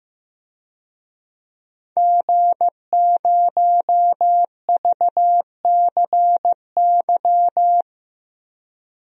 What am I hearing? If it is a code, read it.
Morse code, heard as a keyed tone, about 15 wpm, 700 Hz. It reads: G0VCY
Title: DXCC-100 Call Signs - 5 Characters - 15wpm